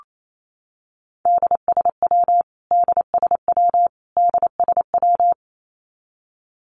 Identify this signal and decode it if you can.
Morse code, heard as a keyed tone, about 28 wpm, 700 Hz. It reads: DSW DSW DSW
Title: Abbreviations Repeated 3x - 28wpm